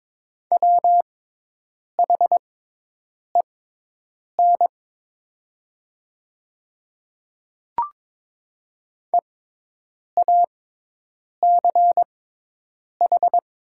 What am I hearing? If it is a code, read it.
Morse code, heard as a keyed tone, about 22 wpm, spaced as for 8 wpm, 700 Hz. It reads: WHEN EACH
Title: 6x Character Spacing - Top 100 Words - 22wpm